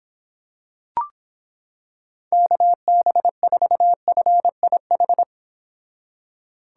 Morse code, heard as a keyed tone, about 26 wpm, 700 Hz. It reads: KB4FIH